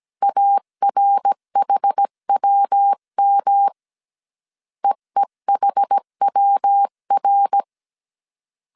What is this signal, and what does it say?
Morse code, heard as a keyed tone, about 17 wpm, 800 Hz. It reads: ARHWM EEHWR
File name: CW / DET / 1 3 1500C S A 16wpm 2141